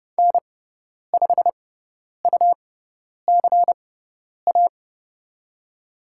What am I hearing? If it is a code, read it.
Morse code, heard as a keyed tone, about 30 wpm, spaced as for 10 wpm, 700 Hz. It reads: N5UCA
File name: Lesson 17g - Letter u - Call Signs 30wpm